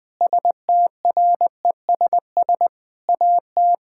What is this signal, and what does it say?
Morse code, heard as a keyed tone, about 20 wpm, 700 Hz. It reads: STRESS AT
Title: Sentences from Top 1000 Words - Review Part 6 - 20wpm